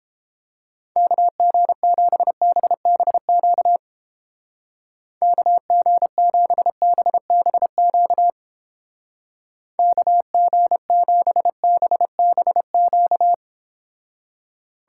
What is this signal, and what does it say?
Morse code, heard as a keyed tone, about 29 wpm, 700 Hz. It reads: KG7BBQ KG7BBQ KG7BBQ